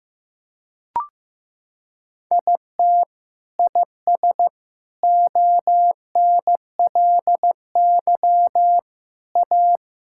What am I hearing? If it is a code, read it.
Morse code, heard as a keyed tone, about 15 wpm, 700 Hz. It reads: IT IS ONLY A
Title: Sentences from Top 1000 Words - Review Part 5 - 15wpm